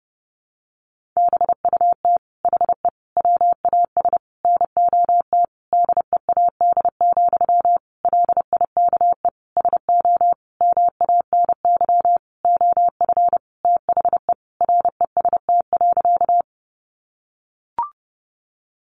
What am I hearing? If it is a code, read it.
Morse code, heard as a keyed tone, about 28 wpm, 700 Hz. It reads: BUT HE WAS NOT DEAD, LIKE SO MANY OF THE REST.
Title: Sentences from Top 1500 Words 30wpm